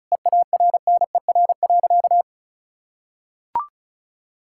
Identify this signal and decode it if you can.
Morse code, heard as a keyed tone, about 36 wpm, 700 Hz. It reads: EARNER.
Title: Pangram Challenge v2 - 35wpm